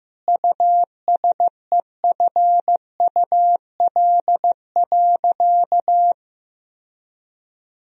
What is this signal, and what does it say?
Morse code, heard as a keyed tone, about 15 wpm, 700 Hz. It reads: USEFUL.
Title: Sentences from Top 1000 Words - Review Part 4 - 15wpm